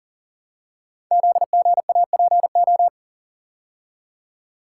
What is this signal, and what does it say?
Morse code, heard as a keyed tone, about 40 wpm, 700 Hz. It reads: ZGAPO
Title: Code Groups - 5 Letters - 40wpm